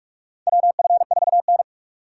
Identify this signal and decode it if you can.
Morse code, heard as a keyed tone, about 46 wpm, 700 Hz. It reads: WF4N
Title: USA Call Signs - 4 Characters - 4x Character Spacing - 45wpm